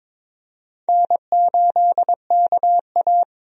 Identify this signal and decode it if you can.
Morse code, heard as a keyed tone, about 22 wpm, 700 Hz. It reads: N8KA
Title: USA Call Signs - 4 Characters - 22wpm